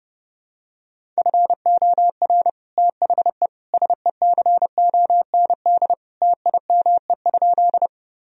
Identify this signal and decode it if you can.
Morse code, heard as a keyed tone, about 32 wpm, 700 Hz. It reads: FOR THE SECOND TIME?
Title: Sentences from Top 1000 Words - Review Part 1 - 30wpm